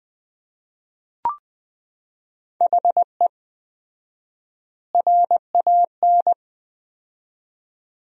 Morse code, heard as a keyed tone, about 20 wpm, 700 Hz. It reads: HE RAN
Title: 4x Word-Spacing - Sentences from Top 500 Words - 20wpm